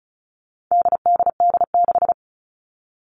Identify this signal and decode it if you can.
Morse code, heard as a keyed tone, about 34 wpm, 700 Hz. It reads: DDD6
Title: All Cut Numbers - 4-Digit 35wpm